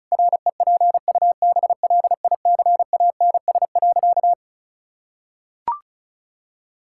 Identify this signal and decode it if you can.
Morse code, heard as a keyed tone, about 36 wpm, 700 Hz. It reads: REPUBLICANS.
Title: Pangram Challenge v2 - 35wpm